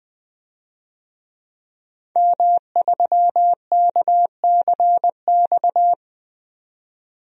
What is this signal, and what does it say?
Morse code, heard as a keyed tone, about 20 wpm, 700 Hz. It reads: M3KCX
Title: DXCC-100 Call Signs - 5 Characters - 20wpm